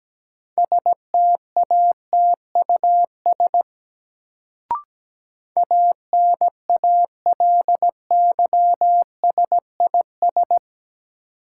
Morse code, heard as a keyed tone, about 17 wpm, 700 Hz. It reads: STATUS ANALYSIS
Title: Sentences from Top 1400 Words - New Word Drill Alt - 17wpm